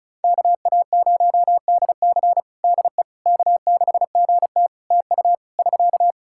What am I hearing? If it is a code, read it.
Morse code, heard as a keyed tone, about 36 wpm, 700 Hz. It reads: KA0DC DE K6GT TU <SK>
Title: Practice QSO - Exchange 4 - Ending - 35wpm